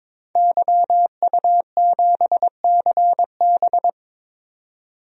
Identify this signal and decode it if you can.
Morse code, heard as a keyed tone, about 21 wpm, 700 Hz. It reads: YU7CB
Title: DXCC-100 Call Signs - 5 Characters - 22wpm